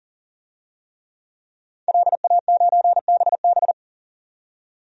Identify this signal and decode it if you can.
Morse code, heard as a keyed tone, about 40 wpm, 700 Hz. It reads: LA9BB
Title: DXCC-100 Call Signs - 5 Characters - 40wpm